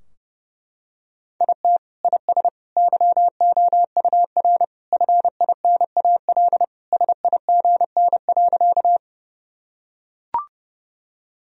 Morse code, heard as a keyed tone, about 30 wpm, 700 Hz. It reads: IT IS YOUR FINAL SIGN.
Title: Sentences from Top 1000 Words - Review Part 2 - 30wpm